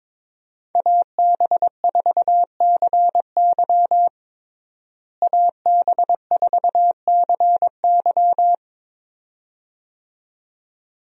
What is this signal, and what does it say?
Morse code, heard as a keyed tone, about 22 wpm, 700 Hz. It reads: AB4CY AB4CY
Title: USA Call Signs - 5 Characters - Repeated 2x - 22wpm